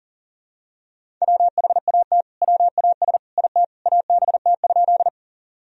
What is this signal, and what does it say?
Morse code, heard as a keyed tone, about 38 wpm, 700 Hz. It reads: WHAT WAS IT ABT?
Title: Sentences from Top 100 Words and Abbreviations 40wpm